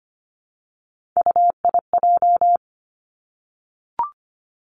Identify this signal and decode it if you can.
Morse code, heard as a keyed tone, about 25 wpm, 700 Hz. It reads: UIJ
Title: Three Challenging Letter-Number Combinations 25wpm